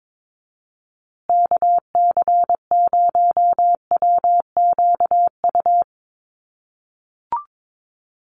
Morse code, heard as a keyed tone, about 22 wpm, 700 Hz. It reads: KC0WQU
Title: USA Call Signs Repeated 2x 22wpm